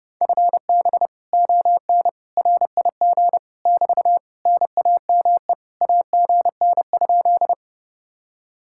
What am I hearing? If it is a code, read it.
Morse code, heard as a keyed tone, about 30 wpm, 700 Hz. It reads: FB ON RIG <BT> NAME AGN?